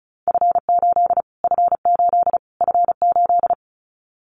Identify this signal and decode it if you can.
Morse code, heard as a keyed tone, about 34 wpm, 700 Hz. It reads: F8 F8 F8
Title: Callsign Prefixes for Top 25 DXCC - Repeated 3x - 35wpm